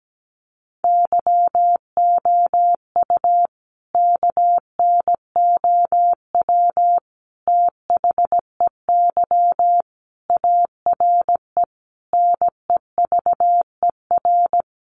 Morse code, heard as a keyed tone, about 17 wpm, 700 Hz. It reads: YOU KNOW THEY ARE NEVER
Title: Sentences from Top 1400 Words 17wpm